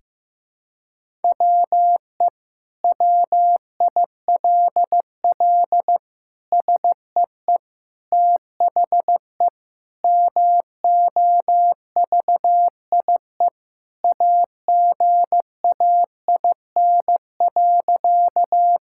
Morse code, heard as a keyed tone, about 15 wpm, 700 Hz. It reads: WE WILL SEE THE MOVIE AGAIN.
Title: Sentences from Top 1000 Words - Review Part 5 - 15wpm